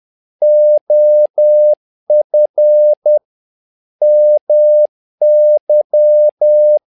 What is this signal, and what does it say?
Morse code, heard as a keyed tone, about 10 wpm, 600 Hz. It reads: OF MY
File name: learning center / tower trouble / tower trouble 10wpm